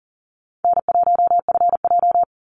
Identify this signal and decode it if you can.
Morse code, heard as a keyed tone, about 40 wpm, 700 Hz. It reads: N1FJ
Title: NA SOTA Activator Callsigns 40wpm